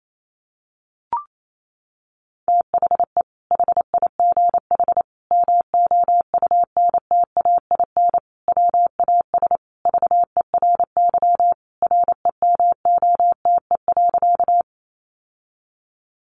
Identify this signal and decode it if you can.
Morse code, heard as a keyed tone, about 28 wpm, 700 Hz. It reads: THE HIGH MOUNTAIN WAS VERY REMOTE.
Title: Sentences from Top 1300 Words 28wpm